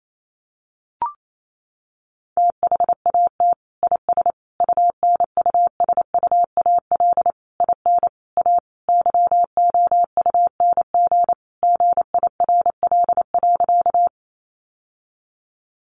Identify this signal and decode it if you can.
Morse code, heard as a keyed tone, about 28 wpm, 700 Hz. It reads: THAT IS UNUSUAL IN A YOUNG GIRL.
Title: Sentences from Top 1400 Words 28wpm